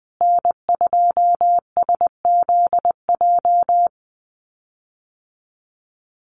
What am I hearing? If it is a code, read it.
Morse code, heard as a keyed tone, about 20 wpm, 700 Hz. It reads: N2SZJ
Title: USA Call Signs Repeated 2x 20wpm